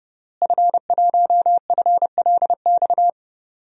Morse code, heard as a keyed tone, about 32 wpm, 700 Hz. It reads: F1FLX
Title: DXCC-100 Call Signs - 5 Characters - Repeated 3x - 30wpm